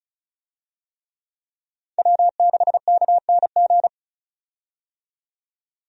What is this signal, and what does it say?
Morse code, heard as a keyed tone, about 34 wpm, 700 Hz. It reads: W6KNG
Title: USA Call Signs - 5 Characters - 35wpm